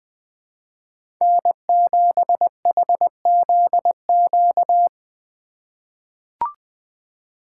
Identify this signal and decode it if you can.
Morse code, heard as a keyed tone, about 20 wpm, 700 Hz. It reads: N7HZQ
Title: USA Call Signs - 5 Characters - 3x Character Spacing - 20wpm